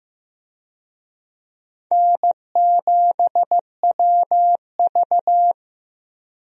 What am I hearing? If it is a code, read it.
Morse code, heard as a keyed tone, about 15 wpm, 700 Hz. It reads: N7WV
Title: USA Call Signs - 4 Characters - Repeated 2x - 15wpm